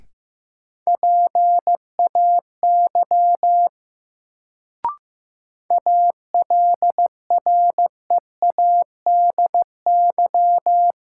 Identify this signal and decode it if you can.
Morse code, heard as a keyed tone, about 15 wpm, 700 Hz. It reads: PAY ALREADY
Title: Sentences from Top 700 Words - New Word Drill - 15wpm